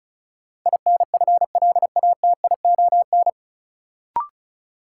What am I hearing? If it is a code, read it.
Morse code, heard as a keyed tone, about 36 wpm, 700 Hz. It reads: INFLATION
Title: Sentences from Top 1100 Words - New Word Drill - 35wpm